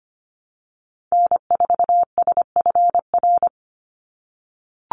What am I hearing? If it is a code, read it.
Morse code, heard as a keyed tone, about 25 wpm, 700 Hz. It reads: N4SFR
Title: NA SOTA Activator Callsigns 25wpm